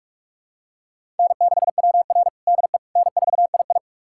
Morse code, heard as a keyed tone, about 48 wpm, 700 Hz. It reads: N6WR DE N4II
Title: USA Callsign DE Callsign - 4 Characters - 45wpm